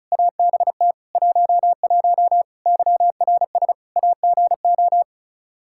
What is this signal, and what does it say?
Morse code, heard as a keyed tone, about 34 wpm, 700 Hz. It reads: ABT 11 YRS AGO
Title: Practice QSO - Exchange 3 - More Info - 35wpm